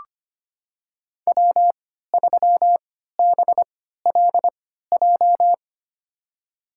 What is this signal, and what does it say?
Morse code, heard as a keyed tone, about 25 wpm, spaced as for 14 wpm, 700 Hz. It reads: W3BLJ
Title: USA Call Signs - 3x Character Spacing - 25wpm